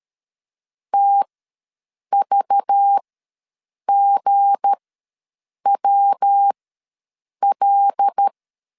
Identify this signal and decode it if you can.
Morse code, heard as a keyed tone, about 13 wpm, spaced as for 7 wpm, 800 Hz. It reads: TVGWL
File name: LEZIONI / 8 YJ45 / 8 YJ ws 12wpm 3342